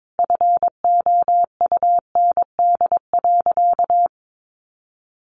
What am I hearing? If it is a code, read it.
Morse code, heard as a keyed tone, about 22 wpm, 700 Hz. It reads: FOUND.
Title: Sentences from Top 1500 Words 22wpm